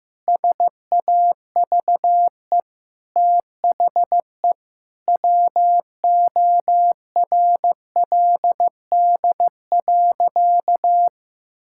Morse code, heard as a keyed tone, about 15 wpm, 700 Hz. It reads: SAVE THE WORLD.
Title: Sentences from Top 1600 Words 15wpm